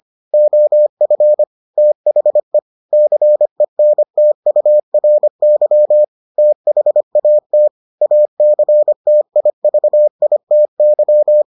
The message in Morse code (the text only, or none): OF THE CENTURY THAT ACTIVITY